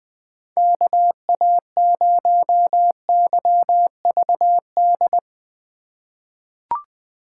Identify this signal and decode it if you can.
Morse code, heard as a keyed tone, about 20 wpm, 700 Hz. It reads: KA0YVD
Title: USA Call Signs - 6 Characters - 2x Character Spacing - 20wpm